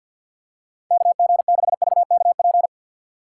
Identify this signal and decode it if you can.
Morse code, heard as a keyed tone, about 48 wpm, 700 Hz. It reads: KG6VKP
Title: USA Call Signs - 6 Characters - 6x Character Spacing - 50wpm